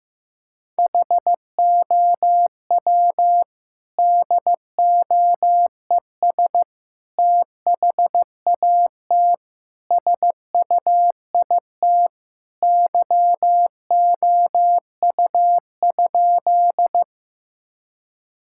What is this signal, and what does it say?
Morse code, heard as a keyed tone, about 15 wpm, 700 Hz. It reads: HOW DOES THAT SUIT YOU?
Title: Sentences from Top 1600 Words 15wpm